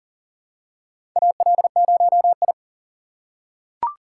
Morse code, heard as a keyed tone, about 41 wpm, 700 Hz. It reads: AL0I
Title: USA Call Signs - 4 Characters - 4x Character Spacing - 40wpm